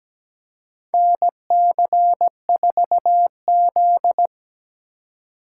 Morse code, heard as a keyed tone, about 17 wpm, 700 Hz. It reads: NC4Z
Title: USA Call Signs - 4 Characters - 4x Character Spacing - 17wpm